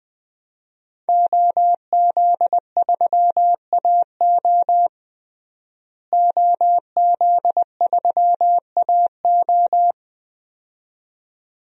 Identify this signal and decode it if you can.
Morse code, heard as a keyed tone, about 20 wpm, 700 Hz. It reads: OZ3AO OZ3AO
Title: DXCC-100 Call Signs - 5 Characters - Repeated 2x - 20wpm